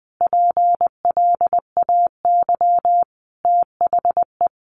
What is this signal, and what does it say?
Morse code, heard as a keyed tone, about 20 wpm, 700 Hz. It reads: PLAY THE
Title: Sentences from Top 1400 Words 20wpm